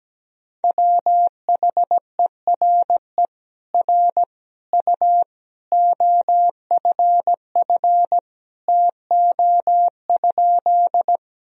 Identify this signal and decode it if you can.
Morse code, heard as a keyed tone, about 17 wpm, 700 Hz. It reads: WHERE R U OFF TO?
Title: Sentences from Top 200 Words and Abbreviations 17wpm